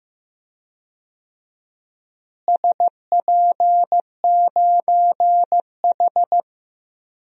Morse code, heard as a keyed tone, about 16 wpm, 700 Hz. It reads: SP9H
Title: DXCC-100 Call Signs - 4 Characters - Repeated 3x - 15wpm